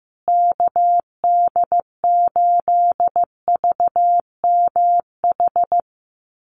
Morse code, heard as a keyed tone, about 15 wpm, 700 Hz. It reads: KD8VMH
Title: USA Call Signs Repeated 3x 15wpm